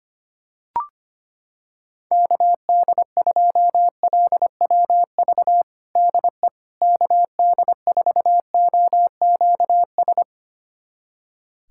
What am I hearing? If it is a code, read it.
Morse code, heard as a keyed tone, about 25 wpm, 700 Hz. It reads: KD2LWV DE KD4OQS
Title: USA Callsign DE Callsign - 6 Characters - 25wpm